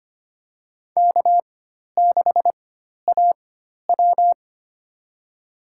Morse code, heard as a keyed tone, about 25 wpm, spaced as for 12 wpm, 700 Hz. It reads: K6AW
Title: USA Call Signs - 4x Character Spacing - 25wpm